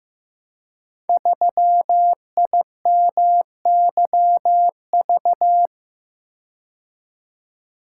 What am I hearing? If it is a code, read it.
Morse code, heard as a keyed tone, about 15 wpm, 700 Hz. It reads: 3IMYV
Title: Code Groups - 5 Letter:Numbers - 15wpm